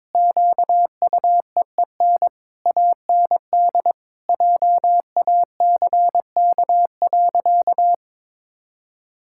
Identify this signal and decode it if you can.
Morse code, heard as a keyed tone, about 22 wpm, 700 Hz. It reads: QUEEN AND JACK.
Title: Pangram Challenge v2 - 22wpm